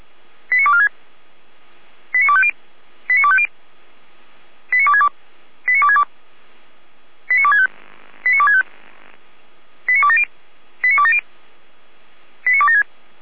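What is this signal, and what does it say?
Five-tone selective call (ZVEI), heard as quick runs of five tones, tones of about 70 ms. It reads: 89246 89250 89250 89262 89262 89265 89265 89270 89270 89276
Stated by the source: ZVEI five-tone selective call received with an SDR, logged on the Feuerwehr Orlamünde audio server.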